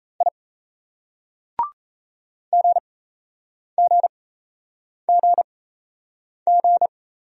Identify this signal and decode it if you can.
Morse code, heard as a keyed tone, about 35 wpm, spaced as for 8 wpm, 700 Hz. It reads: I GGGG
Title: Speed-Racing Single Letters 28wpm